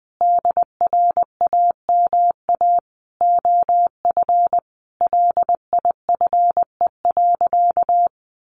Morse code, heard as a keyed tone, about 20 wpm, 700 Hz. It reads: DRAMA OF LIFE.
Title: Sentences from Top 1500 Words 20wpm